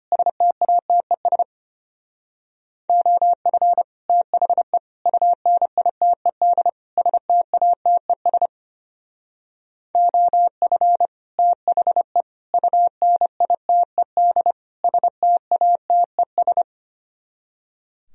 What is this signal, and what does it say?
Morse code, heard as a keyed tone, about 33 wpm, 700 Hz. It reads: STATES OF THE UNITED STATES OF THE UNITED STATES